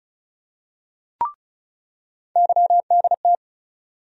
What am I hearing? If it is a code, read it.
Morse code, heard as a keyed tone, about 36 wpm, 700 Hz. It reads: YDT
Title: USA Call Sign Postfixes - 3 Letters 35wpm